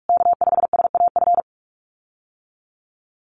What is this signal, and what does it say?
Morse code, heard as a keyed tone, about 44 wpm, 700 Hz. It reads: K5SAF
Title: USA Call Signs Repeated 2x 45wpm